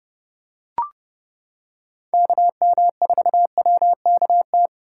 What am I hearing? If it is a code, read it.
Morse code, heard as a keyed tone, about 29 wpm, 700 Hz. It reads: KM4WKT